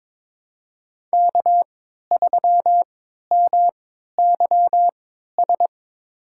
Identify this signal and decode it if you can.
Morse code, heard as a keyed tone, about 22 wpm, spaced as for 12 wpm, 700 Hz. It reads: K3MYS
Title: USA Call Signs - 5 Characters - 3x Character Spacing - 22wpm